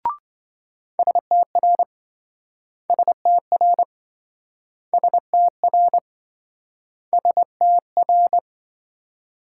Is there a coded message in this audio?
Yes: Morse code from 1.0 to 8.4 s, heard as a keyed tone, about 27 wpm, 700 Hz. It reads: STR STR STR STR